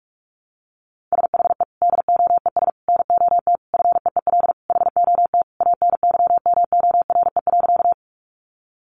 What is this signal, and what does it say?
Morse code, heard as a keyed tone, about 44 wpm, 700 Hz. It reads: SHE DOES NOT FEEL HOT ANYMORE.
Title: Additional Easy Phrases 45wpm